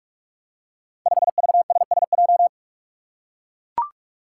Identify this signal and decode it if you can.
Morse code, heard as a keyed tone, about 44 wpm, 700 Hz. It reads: 5VSSJ
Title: Five Challenging Letter-Number Combinations 45wpm